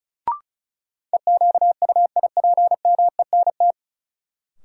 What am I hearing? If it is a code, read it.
Morse code, heard as a keyed tone, about 36 wpm, 700 Hz. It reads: EQUIPMENT
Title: Sentences from Top 700 Words - New Word Drill - 35wpm